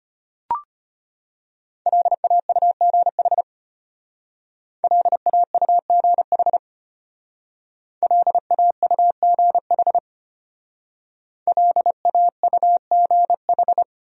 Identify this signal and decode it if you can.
Morse code, heard as a keyed tone, about 32 wpm, 700 Hz. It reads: LAUGH LAUGH LAUGH LAUGH